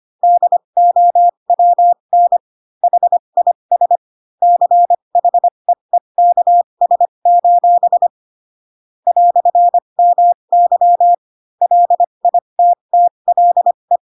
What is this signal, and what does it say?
Morse code, heard as a keyed tone, about 25 wpm, spaced as for 22 wpm, 700 Hz. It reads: DOWN HIS CHEEKS: "MY LITTLE
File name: Pinocchio / Pinocchio ch7 18 25 ik1hge